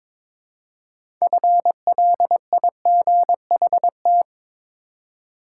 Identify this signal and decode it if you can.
Morse code, heard as a keyed tone, about 22 wpm, 700 Hz. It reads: FLIGHT
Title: Six Letter Words - 22wpm